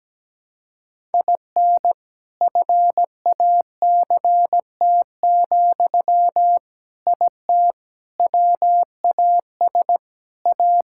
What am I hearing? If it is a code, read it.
Morse code, heard as a keyed tone, about 17 wpm, 700 Hz. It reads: IN FACT, IT WAS A
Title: Sentences from Top 1000 Words - Review Part 6 - 17wpm